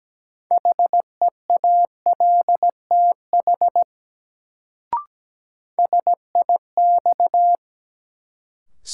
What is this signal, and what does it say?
Morse code, heard as a keyed tone, about 17 wpm, 700 Hz. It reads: HEALTH SIX
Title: Sentences from Top 600 Words - New Word Drill - 17wpm